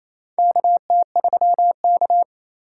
Morse code, heard as a keyed tone, about 28 wpm, 700 Hz. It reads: KT3K
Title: USA Call Signs - 4 Characters - 6x Character Spacing - 28wpm